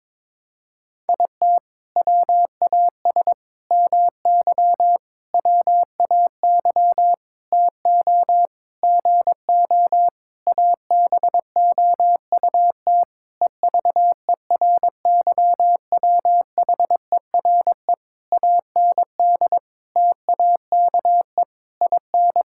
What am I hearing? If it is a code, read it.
Morse code, heard as a keyed tone, about 22 wpm, 700 Hz. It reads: IT WAS MY WAY TO GO ABOUT EVERYWHERE AND TAKE IN
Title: Sentences from Top 1600 Words 22wpm